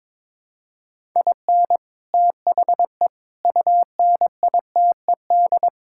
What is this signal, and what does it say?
Morse code, heard as a keyed tone, about 22 wpm, 700 Hz. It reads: IN THE UNITED
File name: practice / 22wpm / Build A Phrase - 5-Word Phrases - 22wpm